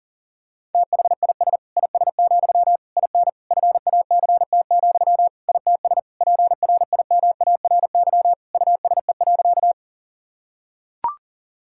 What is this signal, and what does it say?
Morse code, heard as a keyed tone, about 40 wpm, 700 Hz. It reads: THIS IS, IN FACT, ITS PRIMARY USE.
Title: Sentences from Top 1000 Words - Review Part 5 - 40wpm